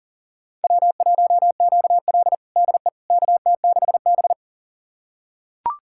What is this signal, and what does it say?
Morse code, heard as a keyed tone, about 37 wpm, 700 Hz. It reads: W1QL DE KT6B
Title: USA Callsign DE Callsign - 4 Characters - 40wpm